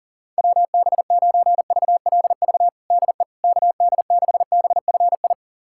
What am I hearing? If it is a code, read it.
Morse code, heard as a keyed tone, about 40 wpm, 700 Hz. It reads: WB9VLV DE KD6BFI
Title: USA Callsign DE Callsign 40wpm